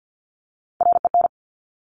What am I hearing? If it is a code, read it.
Morse code, heard as a keyed tone, about 50 wpm, 700 Hz. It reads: REN
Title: Top Three Letter Combinations in Words 50wpm